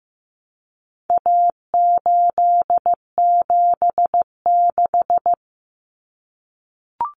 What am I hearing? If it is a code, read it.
Morse code, heard as a keyed tone, about 15 wpm, 700 Hz. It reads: A876
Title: Common Cut Numbers - 4-Digit 15wpm